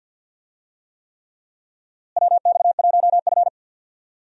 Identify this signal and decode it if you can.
Morse code, heard as a keyed tone, about 52 wpm, 700 Hz. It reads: WX1F
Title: USA Call Signs - 4 Characters - Repeated 3x - 50wpm